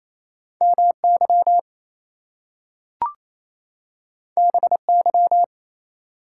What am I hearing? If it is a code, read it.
Morse code, heard as a keyed tone, about 28 wpm, 700 Hz. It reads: MY BY
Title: Two Letter Words - 28wpm